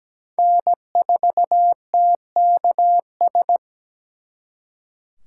Morse code, heard as a keyed tone, about 20 wpm, 700 Hz. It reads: N4TKS
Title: Speed Racing - USA Call Signs 17wpm